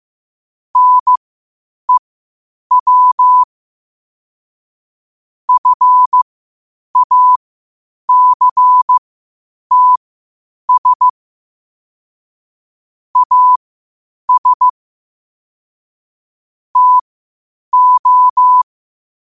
Morse code, heard as a keyed tone, about 15 wpm, spaced as for 8 wpm, 1000 Hz. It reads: NEW FACTS AS TO
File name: qotdfiles / 8wpm